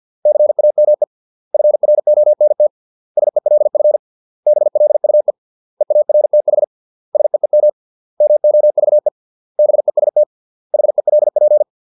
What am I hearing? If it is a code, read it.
Morse code, heard as a keyed tone, about 50 wpm, 600 Hz. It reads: CAME FRONT SELF BLUE EARTH SEEM MOVE BEST HELP